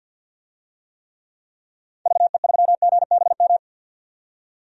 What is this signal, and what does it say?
Morse code, heard as a keyed tone, about 50 wpm, 700 Hz. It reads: VE3GBM
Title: DXCC-100 Call Signs - 6 Characters - 50wpm